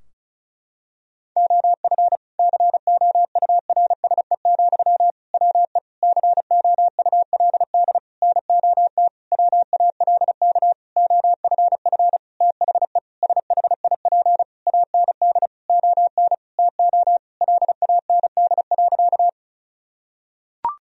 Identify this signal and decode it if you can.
Morse code, heard as a keyed tone, about 34 wpm, 700 Hz. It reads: OF COURSE, WE COULD NOT WALK OFF THE SHIP AND ON TO LAND.
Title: Sentences from Top 1000 Words - Review Part 2 - 35wpm